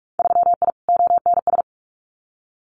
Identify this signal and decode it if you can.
Morse code, heard as a keyed tone, about 46 wpm, 700 Hz. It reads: 3I ONS
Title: Winter Field Day Exchanges Encore 45wpm